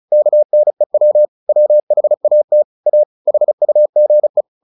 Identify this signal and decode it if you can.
Morse code, heard as a keyed tone, about 34 wpm, 600 Hz. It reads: KNEW WHAT A HUGE
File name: the old man and the sea / the old man and the sea 1 35wpm